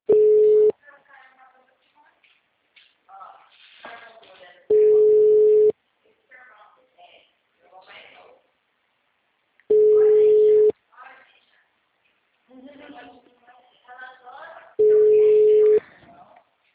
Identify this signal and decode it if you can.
Ringback tone (Europe, 425 Hz), with speech.
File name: out- -2005- - - .58040